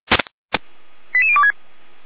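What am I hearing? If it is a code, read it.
Five-tone selective call (ZVEI), heard as quick runs of five tones, tones of about 70 ms. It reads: 88026
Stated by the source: ZVEI five-tone selective call 88103 received by SDR, Feuerwehr Orlamünde fire brigade alerting.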